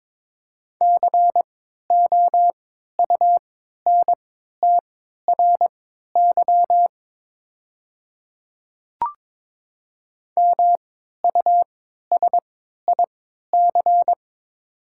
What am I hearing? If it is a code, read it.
Morse code, heard as a keyed tone, about 22 wpm, spaced as for 13 wpm, 700 Hz. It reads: COUNTRY MUSIC
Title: 3x Character Spacing - Top 500 Words - 22wpm